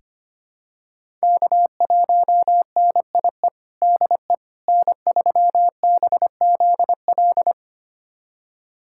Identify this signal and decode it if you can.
Morse code, heard as a keyed tone, about 25 wpm, 700 Hz. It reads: K1NIE DE N3BZL
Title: USA Callsign DE Callsign - 5 Characters - 25wpm